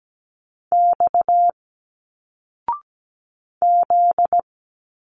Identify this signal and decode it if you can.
Morse code, heard as a keyed tone, about 17 wpm, 700 Hz. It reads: X Z
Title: Single Letters 17wpm